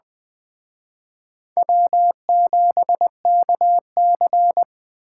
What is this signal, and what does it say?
Morse code, heard as a keyed tone, about 20 wpm, 700 Hz. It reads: W7KC
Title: Speed Racing - USA 1x2 Call Signs 17wpm